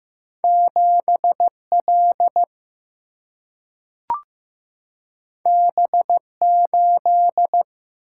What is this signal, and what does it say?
Morse code, heard as a keyed tone, about 15 wpm, 700 Hz. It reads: 7 L B 8
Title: Single-Letter Single-Number 15wpm